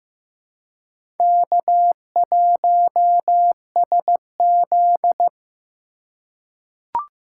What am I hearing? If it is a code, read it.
Morse code, heard as a keyed tone, about 15 wpm, 700 Hz. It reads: K1SZ
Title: USA Call Signs - 4 Characters - 3x Character Spacing - 15wpm